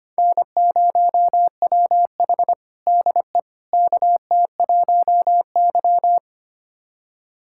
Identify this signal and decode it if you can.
Morse code, heard as a keyed tone, about 25 wpm, 700 Hz. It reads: N0WH DE KT1Y
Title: USA Callsign DE Callsign - 4 Characters - 25wpm